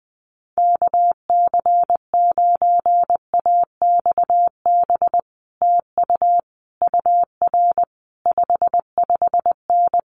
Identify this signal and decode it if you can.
Morse code, heard as a keyed tone, about 20 wpm, 700 Hz. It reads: KC9AXB TU UR 55N
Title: Common QSO Phrases Encore 20wpm